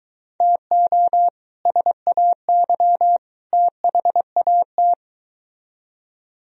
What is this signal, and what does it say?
Morse code, heard as a keyed tone, about 23 wpm, 700 Hz. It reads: TO SAY THAT